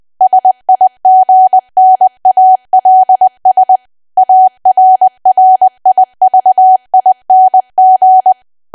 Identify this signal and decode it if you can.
Morse code, heard as a keyed tone, about 20 wpm, 750 Hz. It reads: SIGNALS ARRIVING
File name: cwa / W1AW-Level3 / 20WPM